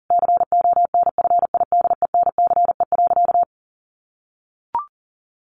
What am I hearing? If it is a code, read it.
Morse code, heard as a keyed tone, about 40 wpm, 700 Hz. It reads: CONFIDENCE.
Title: Sentences from Top 1500 Words 40wpm